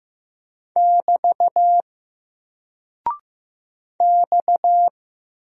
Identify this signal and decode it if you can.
Morse code, heard as a keyed tone, about 15 wpm, 700 Hz. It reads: <BT> X
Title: Single Character (Letter:Number:Punctuation:Prosign) 15wpm